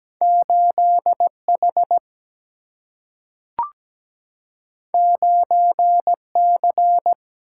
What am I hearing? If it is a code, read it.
Morse code, heard as a keyed tone, about 17 wpm, 700 Hz. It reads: 8H 9C
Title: Single-Letter Single-Number 17wpm